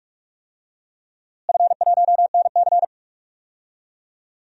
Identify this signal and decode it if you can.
Morse code, heard as a keyed tone, about 46 wpm, 700 Hz. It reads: F1NC
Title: DXCC-100 Call Signs - 4 Characters - Repeated 2x - 45wpm